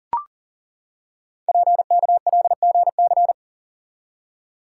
Morse code, heard as a keyed tone, about 39 wpm, 690 Hz. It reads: PKLGC
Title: Code Groups - 5 Letter:Numbers - 40wpm